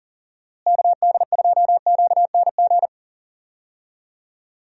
Morse code, heard as a keyed tone, about 39 wpm, 700 Hz. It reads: KD2QNG
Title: USA Call Signs - 6 Characters - Repeated 3x - 40wpm